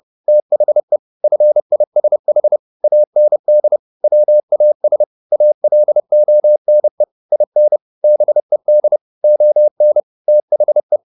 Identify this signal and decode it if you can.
Morse code, heard as a keyed tone, about 30 wpm, 600 Hz. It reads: THE FISH AND WAS ALONE IN BED ON THE